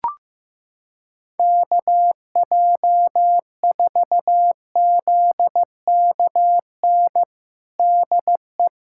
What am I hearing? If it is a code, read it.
Morse code, heard as a keyed tone, about 15 wpm, 700 Hz. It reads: KJ4ZKN DE
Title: USA Callsign DE Callsign - 6 Characters - 15wpm